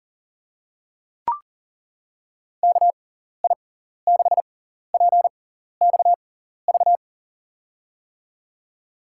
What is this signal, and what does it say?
Morse code, heard as a keyed tone, about 38 wpm, spaced as for 14 wpm, 700 Hz. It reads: KI6PXV
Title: USA Call Signs - 6 Characters - 6x Character Spacing - 40wpm